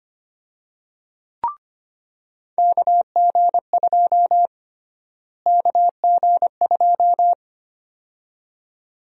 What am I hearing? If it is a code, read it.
Morse code, heard as a keyed tone, about 25 wpm, 700 Hz. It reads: KG2 KG2
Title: USA Call Sign Prefixes - Repeated 2x - 25wpm